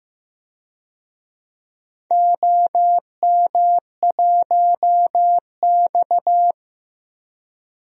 Morse code, heard as a keyed tone, about 15 wpm, 700 Hz. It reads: OM1X
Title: DXCC-100 Call Signs - 4 Characters - 15wpm